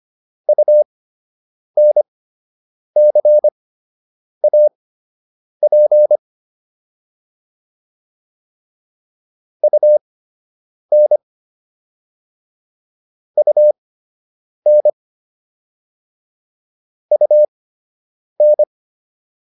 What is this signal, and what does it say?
Morse code, heard as a keyed tone, about 25 wpm, spaced as for 8 wpm, 600 Hz. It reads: UNCAP UN UN UN